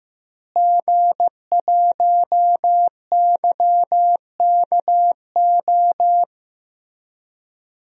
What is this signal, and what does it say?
Morse code, heard as a keyed tone, about 15 wpm, 700 Hz. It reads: G1YKO
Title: DXCC-100 Call Signs - 5 Characters - Repeated 3x - 15wpm